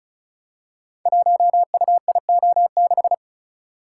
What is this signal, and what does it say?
Morse code, heard as a keyed tone, about 36 wpm, 700 Hz. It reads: 1UIO6
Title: Code Groups - 5 Letter:Numbers - 35wpm